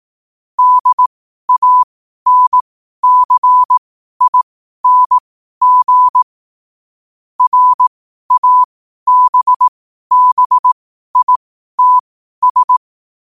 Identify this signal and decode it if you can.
Morse code, heard as a keyed tone, about 18 wpm, spaced as for 12 wpm, 1000 Hz. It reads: DANCING RABBITS
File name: qotdfiles / 12wpm